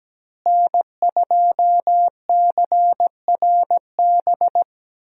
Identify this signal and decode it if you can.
Morse code, heard as a keyed tone, about 17 wpm, 700 Hz. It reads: N2CRB
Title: USA Call Signs - 5 Characters - Repeated 2x - 17wpm